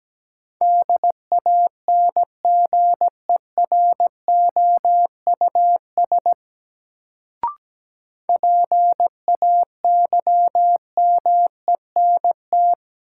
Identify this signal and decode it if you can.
Morse code, heard as a keyed tone, about 17 wpm, 700 Hz. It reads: DANGEROUS PAYMENT
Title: Sentences from Top 1200 Words - New Word Drill - 17wpm